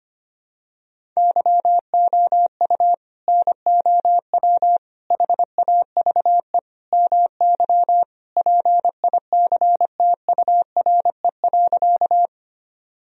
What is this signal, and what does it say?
Morse code, heard as a keyed tone, about 25 wpm, 700 Hz. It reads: YOU NOW HAVE MY PICTURE.
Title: Sentences from Top 1000 Words - Review Part 1 - 25wpm